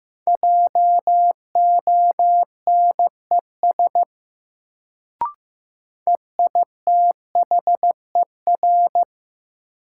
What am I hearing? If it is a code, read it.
Morse code, heard as a keyed tone, about 15 wpm, 700 Hz. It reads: JONES EITHER
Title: Sets of 5 Words - New Word Drill - 15wpm